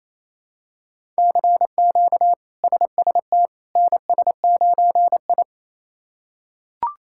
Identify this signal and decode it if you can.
Morse code, heard as a keyed tone, about 28 wpm, 700 Hz. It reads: CQ SST NS9I
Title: K1USN SST QSO Practice - 28wpm